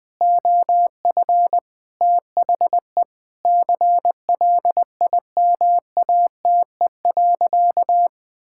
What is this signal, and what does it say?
Morse code, heard as a keyed tone, about 21 wpm, 700 Hz. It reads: OF THE CLIMATE.
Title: Sentences from Top 1600 Words 20wpm